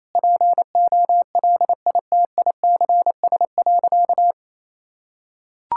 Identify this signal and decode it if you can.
Morse code, heard as a keyed tone, about 28 wpm, 700 Hz. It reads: POLITICS.
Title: Sentences from Top 1000 Words - Review Part 6 - 28wpm